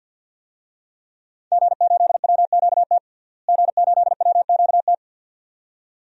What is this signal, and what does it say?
Morse code, heard as a keyed tone, about 47 wpm, 700 Hz. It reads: G8WQT G8WQT
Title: DXCC-100 Call Signs - 5 Characters - Repeated 2x - 50wpm